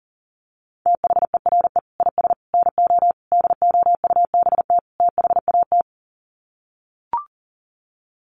Morse code, heard as a keyed tone, about 40 wpm, 700 Hz. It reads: THERE IS NO DOUBT THAT
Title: Sets of 5 Words 40wpm